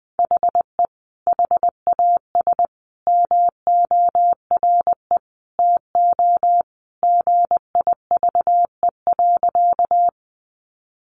Morse code, heard as a keyed tone, about 20 wpm, 700 Hz. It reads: HE HAS MORE TO GIVE.
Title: Sentences from Top 200 Words 20wpm